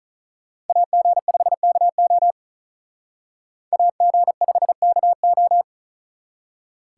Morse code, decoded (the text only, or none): AG5KO AG5KO